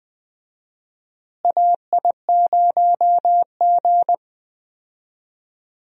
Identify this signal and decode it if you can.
Morse code, heard as a keyed tone, about 20 wpm, 700 Hz. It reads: AI0G
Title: USA Call Signs - 4 Characters - Repeated 3x - 20wpm